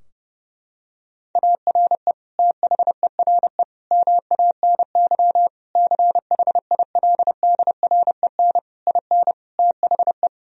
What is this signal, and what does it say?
Morse code, heard as a keyed tone, about 32 wpm, 700 Hz. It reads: ARE THERE MANY CHILDREN IN THE
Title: Sentences from Top 1000 Words - Review Part 4 - 30wpm